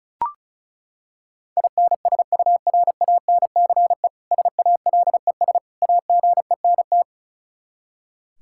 Morse code, heard as a keyed tone, about 36 wpm, 700 Hz. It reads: INSURANCE SALES AGENT
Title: K1USN SST QSO Practice 35wpm